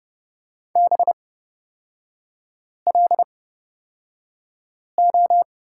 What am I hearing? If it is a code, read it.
Morse code, heard as a keyed tone, about 31 wpm, spaced as for 5 wpm, 690 Hz. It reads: BLO
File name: Lesson 38e - Letter q - Three Characters 30wpm